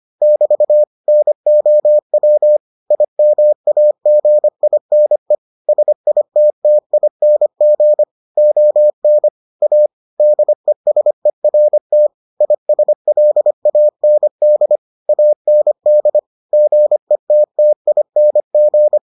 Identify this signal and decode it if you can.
Morse code, heard as a keyed tone, about 25 wpm, 600 Hz. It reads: <BT> NOW IMAGINE SITTING ON A DESERT ISLAND AND GETTING